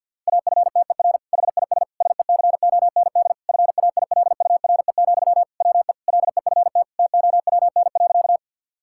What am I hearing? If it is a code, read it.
Morse code, heard as a keyed tone, about 50 wpm, 700 Hz. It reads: AFTER HIS SECOND FAILURE, WE LEFT TOWN.
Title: Sentences from Top 1000 Words - Review Part 6 - 50wpm